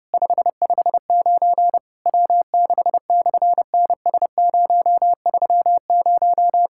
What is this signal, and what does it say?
Morse code, heard as a keyed tone, about 30 wpm, 700 Hz. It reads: 559 W6/NS030
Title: SOTA QSO Practice 30wpm